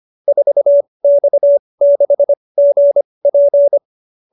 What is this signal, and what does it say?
Morse code, heard as a keyed tone, about 25 wpm, spaced as for 20 wpm, 590 Hz. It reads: 4X6GP